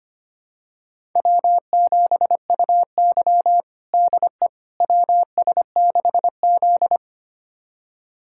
Morse code, heard as a keyed tone, about 25 wpm, 700 Hz. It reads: W7UY DE WS6Z
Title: USA Callsign DE Callsign - 4 Characters - 25wpm